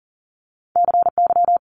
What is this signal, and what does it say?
Morse code, heard as a keyed tone, about 38 wpm, 700 Hz. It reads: CY
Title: Multi-letter Phonemes Quick Fire 40wpm